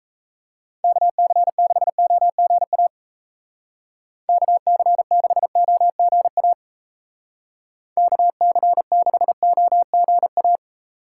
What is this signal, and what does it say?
Morse code, heard as a keyed tone, about 36 wpm, 700 Hz. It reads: KC6OGA KC6OGA KC6OGA